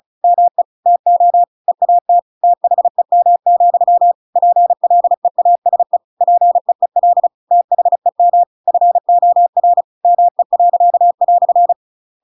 Morse code, heard as a keyed tone, about 35 wpm, 700 Hz. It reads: ME TO EAT THEM, PLEASE PEEL THEM FOR ME."
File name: Pinocchio / Pinocchio ch7 35 35 ik1hge